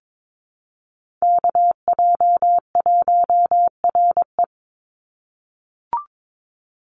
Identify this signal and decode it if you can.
Morse code, heard as a keyed tone, about 22 wpm, 700 Hz. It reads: KJ1RE
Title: Call Signs - 22wpm